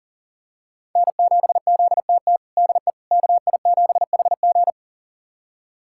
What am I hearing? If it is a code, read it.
Morse code, heard as a keyed tone, about 40 wpm, 700 Hz. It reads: N7ZTT DE KI7HG
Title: USA Callsign DE Callsign - 5 Characters - 40wpm